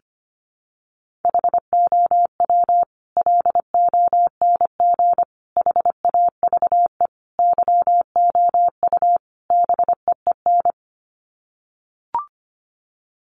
Morse code, heard as a keyed tone, about 25 wpm, 700 Hz. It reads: HOW LONG HAVE YOU BEEN
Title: Sets of 5 Words Encore 25wpm